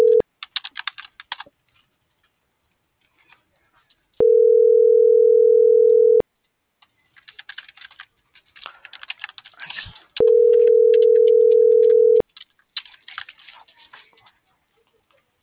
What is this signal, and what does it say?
Ringback tone (North America), 440 + 480 Hz.
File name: CENTRAL MULTIMODAL / out angel.velazquez